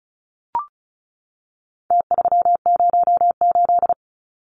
Morse code, heard as a keyed tone, about 34 wpm, 700 Hz. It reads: T308
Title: Numbers - Contest Style - 4-Digit 35wpm